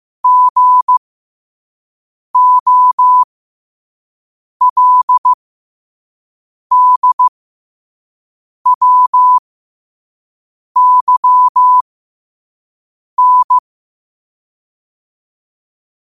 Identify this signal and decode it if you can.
Morse code, heard as a keyed tone, about 15 wpm, spaced as for 5 wpm, 1000 Hz. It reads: GOLDWYN
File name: qotdfiles / 5wpm